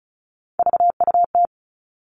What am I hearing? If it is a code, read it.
Morse code, heard as a keyed tone, about 34 wpm, 700 Hz. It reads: VUT
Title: All Cut Numbers - 3-Digit 35wpm